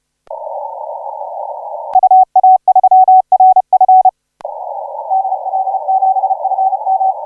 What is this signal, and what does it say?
Morse code, heard as a keyed tone, about 34 wpm, 760 Hz. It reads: UA3RF DL7FM
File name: 5J0R 1807UTC 17m